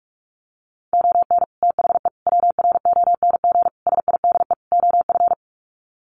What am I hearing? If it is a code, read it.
Morse code, heard as a keyed tone, about 44 wpm, 700 Hz. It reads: ON THE WRONG SIDE OF
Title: Sets of 5 Words Encore 45wpm